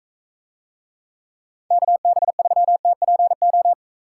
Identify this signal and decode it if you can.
Morse code, heard as a keyed tone, about 41 wpm, 700 Hz. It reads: KB3TPO